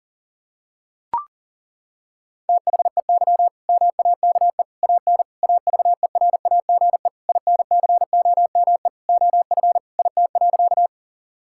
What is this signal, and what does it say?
Morse code, heard as a keyed tone, about 40 wpm, 690 Hz. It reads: THEY MAKE AN AVERAGE INCOME OF IT.
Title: Sentences from Top 1000 Words - Review Part 6 - 40wpm